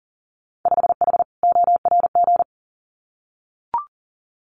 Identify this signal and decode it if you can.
Morse code, heard as a keyed tone, about 39 wpm, 700 Hz. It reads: 5H ORG
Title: Winter Field Day Exchanges Encore 40wpm